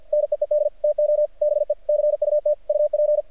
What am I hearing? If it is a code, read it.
Morse code, heard as a keyed tone, about 48 wpm, 600 Hz. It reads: NEED TO BE CUT UP
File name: archived / CRSnet